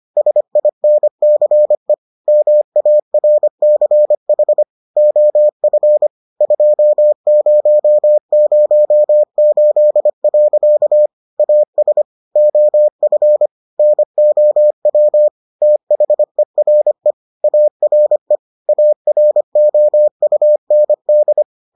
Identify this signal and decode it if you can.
Morse code, heard as a keyed tone, about 25 wpm, 600 Hz. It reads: SINCE MARCH OF 2008. AS OF NOW THERE ARE AROUND